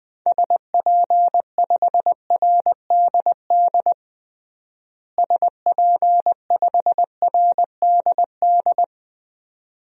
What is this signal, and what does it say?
Morse code, heard as a keyed tone, about 20 wpm, 700 Hz. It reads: SP5RDD SP5RDD
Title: DXCC-100 Call Signs - 6 Characters - Repeated 2x - 20wpm